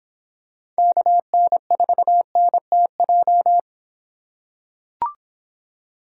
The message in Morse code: KN4NTJ

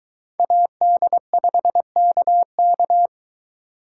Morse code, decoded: AD5KK

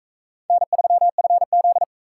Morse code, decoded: N3FZ

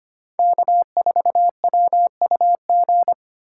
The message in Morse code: K4WUG

